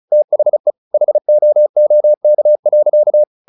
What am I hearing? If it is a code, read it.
Morse code, heard as a keyed tone, about 34 wpm, 600 Hz. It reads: THE HOOK.
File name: the old man and the sea / the old man and the sea 1 35wpm